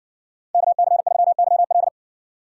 Morse code, heard as a keyed tone, about 53 wpm, 690 Hz. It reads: KC3YL